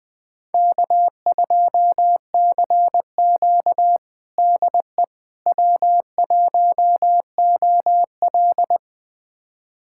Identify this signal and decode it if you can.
Morse code, heard as a keyed tone, about 20 wpm, 700 Hz. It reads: K2CQ DE W1OL
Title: USA Callsign DE Callsign - 4 Characters - 20wpm